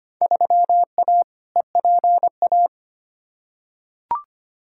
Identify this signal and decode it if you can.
Morse code, heard as a keyed tone, about 25 wpm, 700 Hz. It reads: 3A EPA
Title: Field Day Exchanges - 25wpm